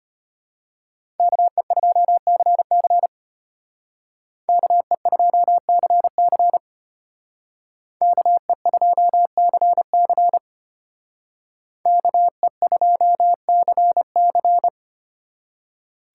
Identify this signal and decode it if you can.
Morse code, heard as a keyed tone, about 32 wpm, 700 Hz. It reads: KE2CC KE2CC KE2CC KE2CC